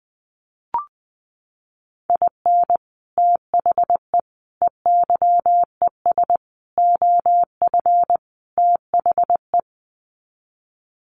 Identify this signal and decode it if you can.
Morse code, heard as a keyed tone, about 20 wpm, 700 Hz. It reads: IN THE EYES OF THE
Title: Sets of 5 Words 20wpm